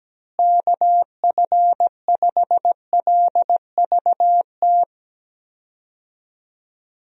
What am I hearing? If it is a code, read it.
Morse code, heard as a keyed tone, about 17 wpm, 700 Hz. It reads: KF5LVT
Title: USA Call Signs - 6 Characters - Repeated 3x - 17wpm